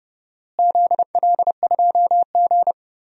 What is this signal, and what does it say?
Morse code, heard as a keyed tone, about 29 wpm, 700 Hz. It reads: ZL2G
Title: DXCC-100 Call Signs - 4 Characters - Repeated 3x - 30wpm